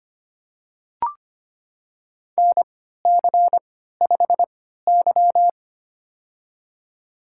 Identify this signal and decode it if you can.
Morse code, heard as a keyed tone, about 25 wpm, spaced as for 14 wpm, 700 Hz. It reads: NC5Y
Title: USA Call Signs - 4 Characters - 3x Character Spacing - 25wpm